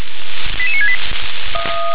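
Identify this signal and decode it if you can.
Five-tone selective call (ZVEI), heard as quick runs of five tones, tones of about 70 ms. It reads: 88968
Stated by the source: ZVEI five-tone call 88778 received by SDR.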